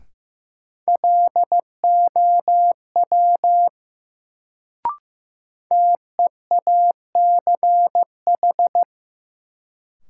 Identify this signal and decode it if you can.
Morse code, heard as a keyed tone, about 15 wpm, 700 Hz. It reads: LOW TEACH
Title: Sentences from Top 500 Words - New Word Drill - 15wpm